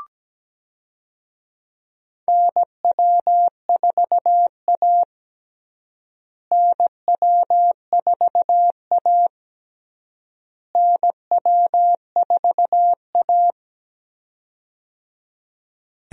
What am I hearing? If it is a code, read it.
Morse code, heard as a keyed tone, about 17 wpm, 700 Hz. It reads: NW4A NW4A NW4A
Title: USA Call Signs - 4 Characters - Repeated 3x - 17wpm